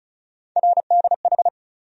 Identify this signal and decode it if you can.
Morse code, heard as a keyed tone, about 36 wpm, 700 Hz. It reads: RDH
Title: USA Call Sign Postfixes - 3 Letters 35wpm